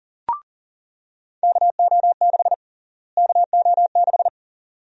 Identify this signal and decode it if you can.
Morse code, heard as a keyed tone, about 40 wpm, 700 Hz. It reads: KO6 KO6
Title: USA Call Sign Prefixes - Repeated 2x - 40wpm